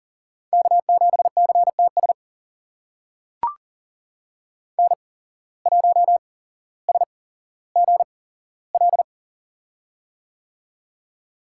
Morse code, heard as a keyed tone, about 40 wpm, spaced as for 17 wpm, 700 Hz. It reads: K7CTS N1SGL
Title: USA Call Signs - 5 Characters - 8x Character Spacing - 40wpm